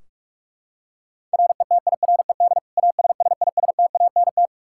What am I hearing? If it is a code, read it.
Morse code, heard as a keyed tone, about 41 wpm, 700 Hz. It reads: RETIRED ASSISTANT
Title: K1USN SST QSO Practice 45wpm